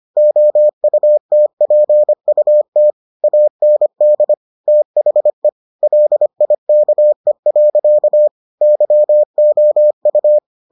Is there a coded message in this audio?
Yes: Morse code from 0.2 to 10.4 s, heard as a keyed tone, about 25 wpm, 600 Hz. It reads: OUTPUT AND THE LIKE. YOU